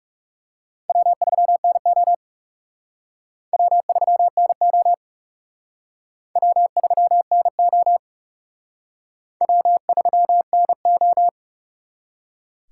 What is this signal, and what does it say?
Morse code, heard as a keyed tone, about 40 wpm, 700 Hz. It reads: W3NO W3NO W3NO W3NO